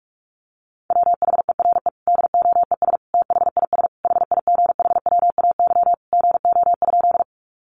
Morse code, heard as a keyed tone, about 44 wpm, 700 Hz. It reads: WHERE DOES THIS HIGHWAY GO?
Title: Sentences from Top 1400 Words 45wpm